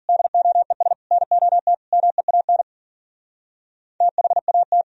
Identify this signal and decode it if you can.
Morse code, heard as a keyed tone, about 52 wpm, 700 Hz. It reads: DOES NOT MEAN THAT